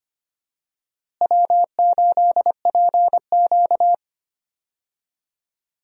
Morse code, heard as a keyed tone, about 25 wpm, 700 Hz. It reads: W8PQ
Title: USA Call Signs - 4 Characters - Repeated 2x - 25wpm